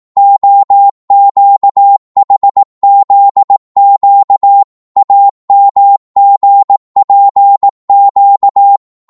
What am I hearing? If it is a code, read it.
Morse code, heard as a keyed tone, about 18 wpm, 800 Hz. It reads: OQHZQ AMGPQ
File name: Morse / ltr-18wpm